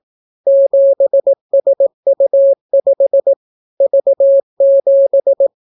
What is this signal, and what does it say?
Morse code, heard as a keyed tone, about 18 wpm, 550 Hz. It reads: 7SU5 V7